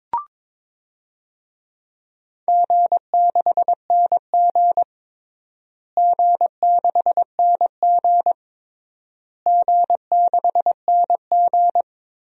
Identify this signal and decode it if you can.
Morse code, heard as a keyed tone, about 22 wpm, 700 Hz. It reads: G6NG G6NG G6NG
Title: DXCC-100 Call Signs - 4 Characters - Repeated 3x - 22wpm